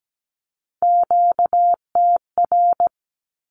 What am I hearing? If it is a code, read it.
Morse code, heard as a keyed tone, about 17 wpm, 700 Hz. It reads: QTR
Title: Q-Codes with Spoken Meaning - 17wpm